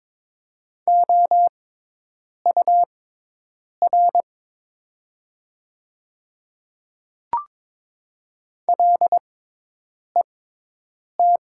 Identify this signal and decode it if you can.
Morse code, heard as a keyed tone, about 22 wpm, spaced as for 8 wpm, 700 Hz. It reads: OUR LET
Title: 6x Character Spacing - Top 300 Words - 22wpm